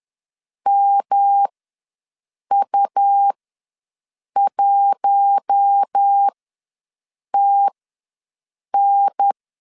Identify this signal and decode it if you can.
Morse code, heard as a keyed tone, about 10 wpm, spaced as for 6 wpm, 800 Hz. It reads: MU1TN
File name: LEZIONI / 5 LBVC / 5 LB ws 10wpm 3616